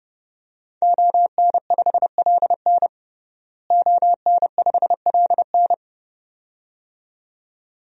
Morse code, heard as a keyed tone, about 30 wpm, 700 Hz. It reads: ON5LN ON5LN
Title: DXCC-100 Call Signs - 5 Characters - Repeated 2x - 30wpm